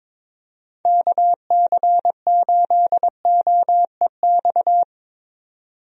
Morse code, heard as a keyed tone, about 22 wpm, 700 Hz. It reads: KC8OEX
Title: USA Call Signs - 6 Characters - 22wpm